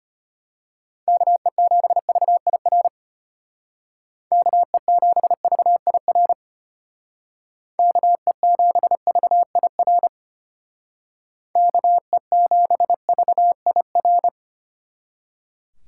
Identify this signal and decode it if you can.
Morse code, heard as a keyed tone, about 32 wpm, 700 Hz. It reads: KE7VIR KE7VIR KE7VIR KE7VIR